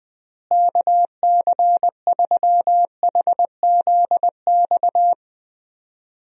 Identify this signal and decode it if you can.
Morse code, heard as a keyed tone, about 21 wpm, 700 Hz. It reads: KC3HZX
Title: USA Call Signs - 6 Characters - Repeated 3x - 20wpm